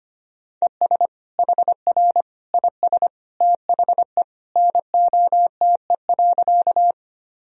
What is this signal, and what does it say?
Morse code, heard as a keyed tone, about 25 wpm, 700 Hz. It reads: ES HR IS THE NOTE.
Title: Sentences from Top 500 Words and Abbreviations 25wpm